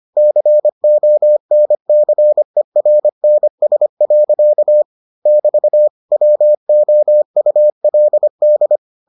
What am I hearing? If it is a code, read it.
Morse code, heard as a keyed tone, about 25 wpm, 600 Hz. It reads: CONCERNS. <BT> WOULD